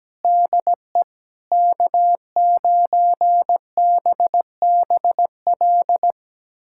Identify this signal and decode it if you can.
Morse code, heard as a keyed tone, about 17 wpm, 700 Hz. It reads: DE K9BBL
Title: USA Callsign DE Callsign - 5 Characters - 17wpm